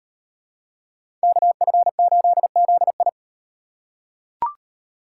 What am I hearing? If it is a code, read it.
Morse code, heard as a keyed tone, about 38 wpm, 700 Hz. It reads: KF8ZI